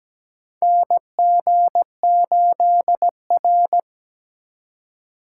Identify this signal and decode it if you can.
Morse code, heard as a keyed tone, about 17 wpm, 700 Hz. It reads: NG8R
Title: USA Call Signs - 4 Characters - 8x Character Spacing - 17wpm